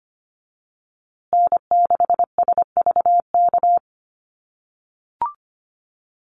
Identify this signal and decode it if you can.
Morse code, heard as a keyed tone, about 25 wpm, 700 Hz. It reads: N6SVK
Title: USA Call Signs Repeated 2x 25wpm